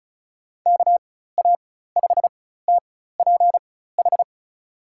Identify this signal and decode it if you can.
Morse code, heard as a keyed tone, about 36 wpm, spaced as for 16 wpm, 700 Hz. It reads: KA5TPH
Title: USA Call Signs - 6 Characters - 4x Character Spacing - 35wpm